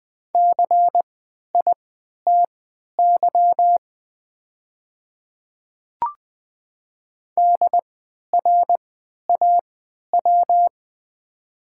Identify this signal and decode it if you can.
Morse code, heard as a keyed tone, about 20 wpm, spaced as for 11 wpm, 700 Hz. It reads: CITY DRAW
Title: 3x Character Spacing - Top 500 Words - 20wpm